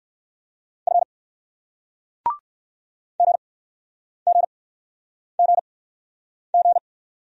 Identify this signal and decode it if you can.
Morse code, heard as a keyed tone, about 57 wpm, 700 Hz. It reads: F G G G G